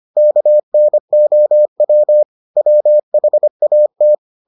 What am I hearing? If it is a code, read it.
Morse code, heard as a keyed tone, about 25 wpm, 600 Hz. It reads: KNOW WHAT